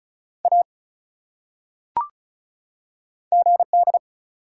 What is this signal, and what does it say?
Morse code, heard as a keyed tone, about 34 wpm, 700 Hz. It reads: A GD
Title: Top 180 QSO Elements 35wpm